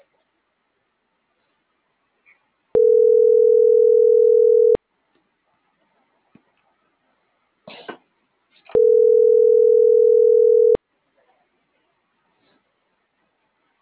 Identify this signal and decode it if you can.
Ringback tone (North America), 440 + 480 Hz.